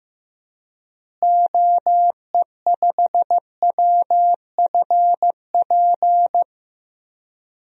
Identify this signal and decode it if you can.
Morse code, heard as a keyed tone, about 16 wpm, 700 Hz. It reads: OE5WFP
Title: DXCC-100 Call Signs - 6 Characters - Repeated 2x - 15wpm